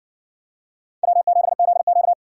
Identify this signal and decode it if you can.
Morse code, heard as a keyed tone, about 56 wpm, 700 Hz. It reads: W7ZQ